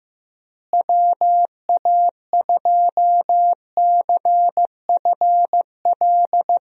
Morse code, heard as a keyed tone, about 16 wpm, 700 Hz. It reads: WA2CFL
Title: USA Call Signs - 6 Characters - Repeated 3x - 15wpm